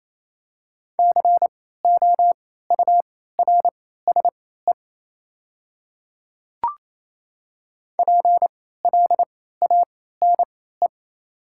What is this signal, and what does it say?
Morse code, heard as a keyed tone, about 28 wpm, spaced as for 16 wpm, 700 Hz. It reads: COURSE PLANE
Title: 3x Character Spacing - Top 500 Words - 28wpm